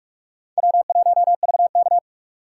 Morse code, heard as a keyed tone, about 44 wpm, 700 Hz. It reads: W1VK